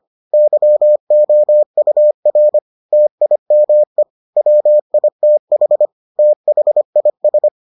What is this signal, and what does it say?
Morse code, heard as a keyed tone, about 25 wpm, 600 Hz. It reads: YOUR TIME WITH THIS